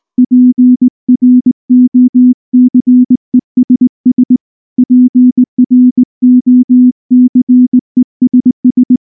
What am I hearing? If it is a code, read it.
Morse code, heard as a keyed tone, about 19 wpm, 250 Hz. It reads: PROCESS PROCESS
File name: morse / morse-training-20WPM-5